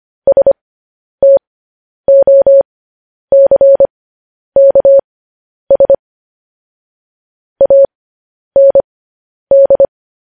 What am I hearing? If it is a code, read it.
Morse code, heard as a keyed tone, about 25 wpm, spaced as for 10 wpm, 550 Hz. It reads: STOCKS AND
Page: News Headlines in Morse Code at 10 WPM